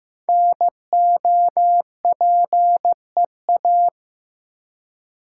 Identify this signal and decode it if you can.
Morse code, heard as a keyed tone, about 15 wpm, 700 Hz. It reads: NOPEA
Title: Code Groups - 5 Letters - 15wpm